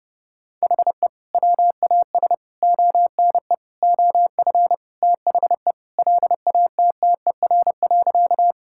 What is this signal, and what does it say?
Morse code, heard as a keyed tone, about 30 wpm, 700 Hz. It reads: HE WAS ONE OF THE LATTER.
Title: Sentences from Top 1600 Words 30wpm